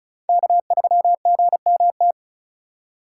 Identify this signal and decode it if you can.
Morse code, heard as a keyed tone, about 36 wpm, 700 Hz. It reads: K3GMT
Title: K1USN SST QSO Practice - 35wpm